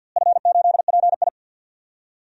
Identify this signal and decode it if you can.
Morse code, heard as a keyed tone, about 47 wpm, 700 Hz. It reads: F8PI
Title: DXCC-100 Call Signs - 4 Characters - Repeated 2x - 50wpm